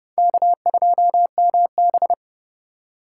Morse code, heard as a keyed tone, about 38 wpm, 700 Hz. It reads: K2MB